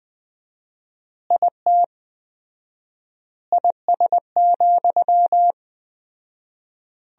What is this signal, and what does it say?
Morse code, heard as a keyed tone, about 20 wpm, 700 Hz. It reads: IT IS,
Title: 4x Word-Spacing - Sentences from Top 200 Words - 20wpm